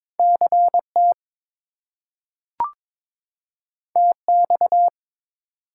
Morse code, heard as a keyed tone, about 22 wpm, 700 Hz. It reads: CT TX
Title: US States and Canadian Provinces 22wpm